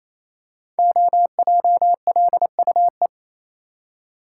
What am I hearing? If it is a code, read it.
Morse code, heard as a keyed tone, about 27 wpm, 700 Hz. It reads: OJLUE
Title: Code Groups - 5 Letter:Numbers - 28wpm